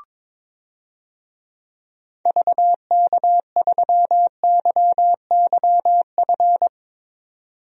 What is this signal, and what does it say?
Morse code, heard as a keyed tone, about 22 wpm, 700 Hz. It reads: VK3YYF
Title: DXCC-100 Call Signs - 6 Characters - 22wpm